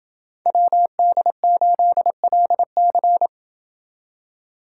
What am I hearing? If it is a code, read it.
Morse code, heard as a keyed tone, about 27 wpm, 700 Hz. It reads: WD8LC